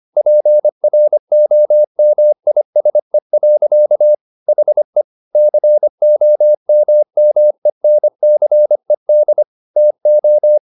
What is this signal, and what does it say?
Morse code, heard as a keyed tone, about 25 wpm, 600 Hz. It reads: PROMISE. HE COMMENCED TO
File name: the old man and the sea part 1 25wpm